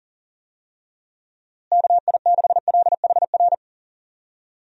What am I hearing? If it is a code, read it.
Morse code, heard as a keyed tone, about 40 wpm, 700 Hz. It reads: KI6LHR
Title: USA Call Signs - 6 Characters - Repeated 3x - 40wpm